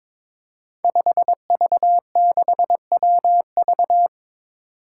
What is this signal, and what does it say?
Morse code, heard as a keyed tone, about 22 wpm, 700 Hz. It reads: 5V6WV
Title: Five Challenging Letter-Number Combinations 22wpm